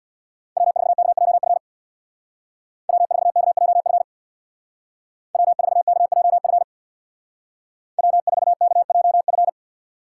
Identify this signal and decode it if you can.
Morse code, heard as a keyed tone, about 64 wpm, 700 Hz. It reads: W4KJF W4KJF W4KJF W4KJF